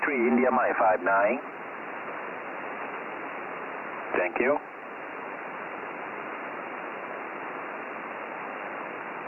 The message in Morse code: none